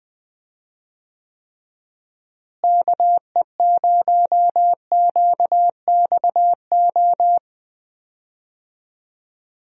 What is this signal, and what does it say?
Morse code, heard as a keyed tone, about 20 wpm, 700 Hz. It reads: KE0QXO
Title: USA Call Signs - 6 Characters - Repeated 3x - 20wpm